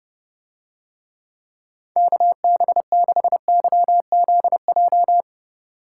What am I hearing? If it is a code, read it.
Morse code, heard as a keyed tone, about 29 wpm, 700 Hz. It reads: KB6YZJ